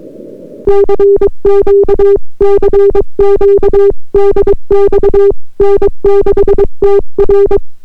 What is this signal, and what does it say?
Morse code, heard as a keyed tone, about 22 wpm, 400 Hz. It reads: CQ CQ DX N6TR